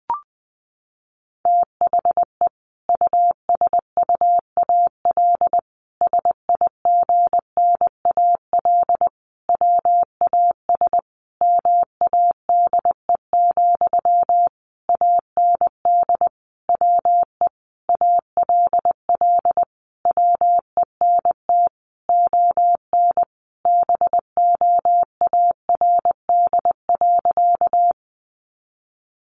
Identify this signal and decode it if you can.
Morse code, heard as a keyed tone, about 20 wpm, 700 Hz. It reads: THE USUAL SIGNAL WAS MADE, AND WE ALL WENT ON BOARD.
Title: Sentences from Top 1400 Words 20wpm